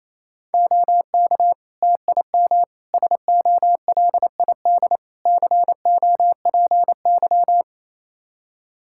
Morse code, heard as a keyed tone, about 28 wpm, 700 Hz. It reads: OK TIM SOLID COPY
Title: Practice QSO - Exchange 3 - More Info - 28wpm